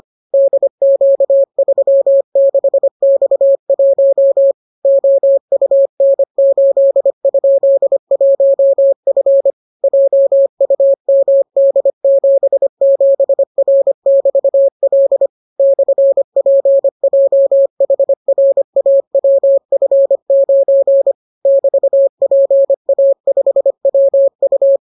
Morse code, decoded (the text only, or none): DQ36X1 OUN8?1F JUMD77R<BT>L /PJHRAWF9 <BT>PA5WU